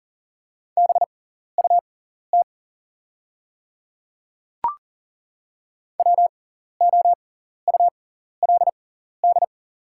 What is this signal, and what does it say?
Morse code, heard as a keyed tone, about 40 wpm, spaced as for 14 wpm, 700 Hz. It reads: BUT WOULD
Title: 6x Character Spacing - Top 200 Words - 40wpm